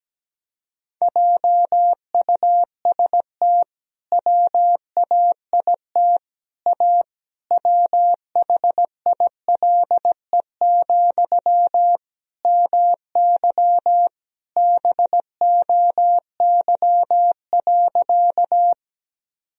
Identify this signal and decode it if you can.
Morse code, heard as a keyed tone, about 17 wpm, 700 Hz. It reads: JUST WAIT A WHILE, MY BOY.
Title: Sentences from Top 1000 Words - Review Part 1 - 17wpm